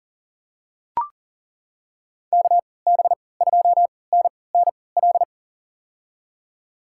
Morse code, heard as a keyed tone, about 40 wpm, spaced as for 23 wpm, 700 Hz. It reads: KB2NNL
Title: USA Call Signs - 6 Characters - 3x Character Spacing - 40wpm